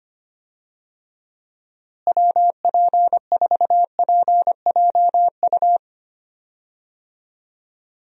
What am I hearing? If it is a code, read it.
Morse code, heard as a keyed tone, about 25 wpm, 700 Hz. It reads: WP4PJU
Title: USA Call Signs - 6 Characters - Repeated 2x - 25wpm